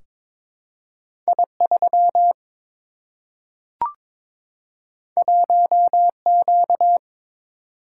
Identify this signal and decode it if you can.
Morse code, heard as a keyed tone, about 22 wpm, 700 Hz. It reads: I3 1Q
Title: Single-Letter Single-Number 22wpm